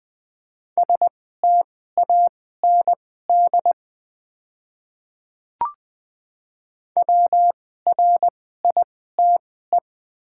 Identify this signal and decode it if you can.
Morse code, heard as a keyed tone, about 20 wpm, spaced as for 14 wpm, 700 Hz. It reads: STAND WRITE
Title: 2x Character Spacing - Top 300 Words - 20wpm